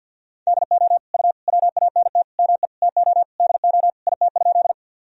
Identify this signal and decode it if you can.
Morse code, heard as a keyed tone, about 52 wpm, 700 Hz. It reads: DO U WANT ME TO DO IT?
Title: Sentences from Top 200 Words and Abbreviations 50wpm